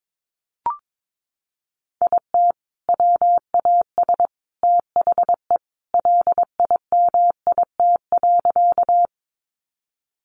Morse code, heard as a keyed tone, about 22 wpm, 700 Hz. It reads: IT WAS THE LIMIT.
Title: Sentences from Top 1500 Words 22wpm